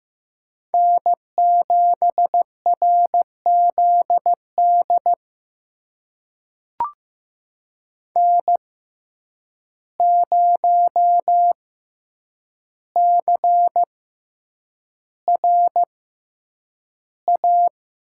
Morse code, heard as a keyed tone, about 15 wpm, spaced as for 8 wpm, 700 Hz. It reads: N7RZD N0CRA
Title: USA Call Signs - 5 Characters - 6x Character Spacing - 15wpm